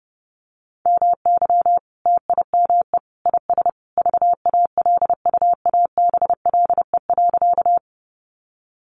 Morse code, heard as a keyed tone, about 30 wpm, 700 Hz. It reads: MY TIME IS VALUABLE.
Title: Sentences from Top 1100 Words 30wpm